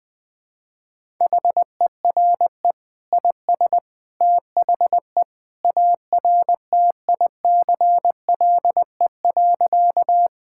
Morse code, heard as a keyed tone, about 19 wpm, 700 Hz. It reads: HERE IS THE ARTICLE.
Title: Sentences from Top 1000 Words - Review Part 5 - 20wpm